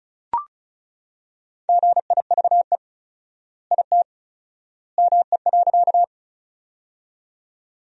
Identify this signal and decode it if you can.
Morse code, heard as a keyed tone, about 36 wpm, 700 Hz. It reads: GIVE IT ME.
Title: 4x Word-Spacing - Sentences from Top 200 Words - 35wpm